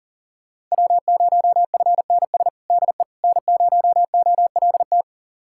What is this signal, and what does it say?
Morse code, heard as a keyed tone, about 37 wpm, 690 Hz. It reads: W0FNS DE N0OLT
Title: USA Callsign DE Callsign - 5 Characters - 40wpm